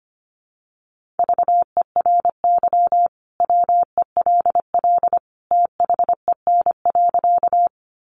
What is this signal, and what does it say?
Morse code, heard as a keyed tone, about 25 wpm, 700 Hz. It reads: VERY WELL THEN.
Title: Sentences from Top 200 Words 25wpm